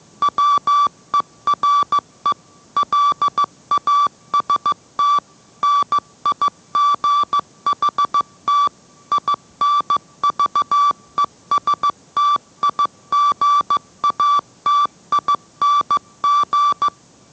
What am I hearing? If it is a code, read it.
Morse code, heard as a keyed tone, about 15 wpm, 1160 Hz. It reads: WERE LAST NIGHT INVESTIGATING